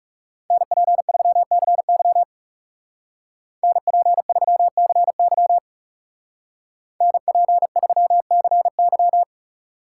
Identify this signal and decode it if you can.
Morse code, heard as a keyed tone, about 39 wpm, 700 Hz. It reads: NP3CY NP3CY NP3CY